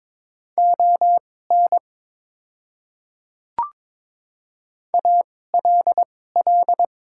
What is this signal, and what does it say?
Morse code, heard as a keyed tone, about 22 wpm, spaced as for 16 wpm, 700 Hz. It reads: ON ALL
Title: 2x Character Spacing - Top 100 Words - 22wpm